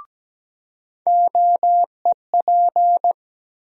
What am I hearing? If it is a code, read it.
Morse code, heard as a keyed tone, about 17 wpm, 700 Hz. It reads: OEP
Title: USA Call Sign Postfixes - 3 Letters 17wpm